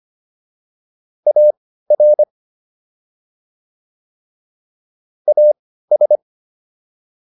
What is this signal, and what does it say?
Morse code, heard as a keyed tone, about 25 wpm, spaced as for 15 wpm, 600 Hz. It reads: AR AS